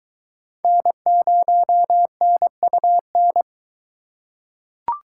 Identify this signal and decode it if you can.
Morse code, heard as a keyed tone, about 22 wpm, 700 Hz. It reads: N0NUN